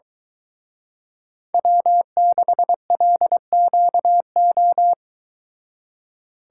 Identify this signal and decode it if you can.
Morse code, heard as a keyed tone, about 23 wpm, 700 Hz. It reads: W6LQO